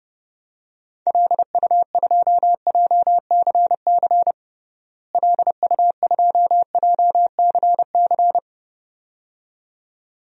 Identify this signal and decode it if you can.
Morse code, heard as a keyed tone, about 31 wpm, 700 Hz. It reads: LU2JCC LU2JCC
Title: DXCC-100 Call Signs - 6 Characters - Repeated 2x - 30wpm